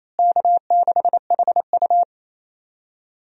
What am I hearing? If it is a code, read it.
Morse code, heard as a keyed tone, about 28 wpm, 700 Hz. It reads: K6HU
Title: USA Call Signs - 4 Characters - Repeated 3x - 28wpm